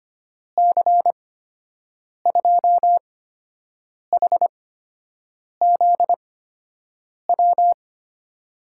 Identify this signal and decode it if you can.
Morse code, heard as a keyed tone, about 25 wpm, spaced as for 7 wpm, 700 Hz. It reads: C2HZW
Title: USA Call Signs - 6 Characters - 8x Character Spacing - 25wpm